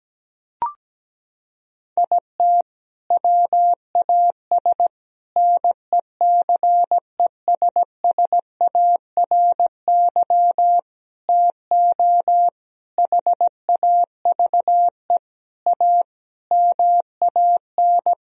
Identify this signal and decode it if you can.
Morse code, heard as a keyed tone, about 17 wpm, 700 Hz. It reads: IT WAS NECESSARY TO HAVE A MAN
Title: Sentences from Top 1000 Words - Review Part 1 - 17wpm